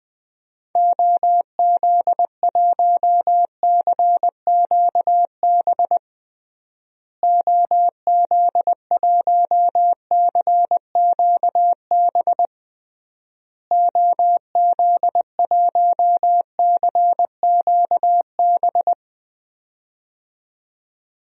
Morse code, heard as a keyed tone, about 20 wpm, 700 Hz. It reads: OZ1CQB OZ1CQB OZ1CQB
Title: DXCC-100 Call Signs - 6 Characters - Repeated 3x - 20wpm